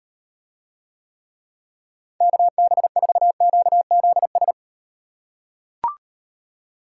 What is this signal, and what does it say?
Morse code, heard as a keyed tone, about 36 wpm, 700 Hz. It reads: KB4QZS